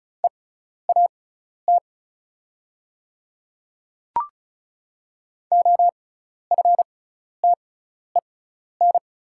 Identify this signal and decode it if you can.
Morse code, heard as a keyed tone, about 36 wpm, spaced as for 12 wpm, 690 Hz. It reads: EAT OFTEN
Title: 6x Character Spacing - Top 300 Words - 35wpm